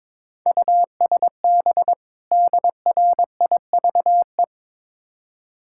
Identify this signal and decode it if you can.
Morse code, heard as a keyed tone, about 22 wpm, 700 Hz. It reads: USB DRIVE
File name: Easy Words - Speed-Racing with Hints - 22wpm